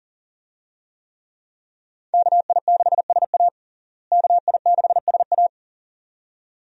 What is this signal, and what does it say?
Morse code, heard as a keyed tone, about 40 wpm, 700 Hz. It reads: KI6SA KI6SA
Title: USA Call Signs - 5 Characters - Repeated 2x - 40wpm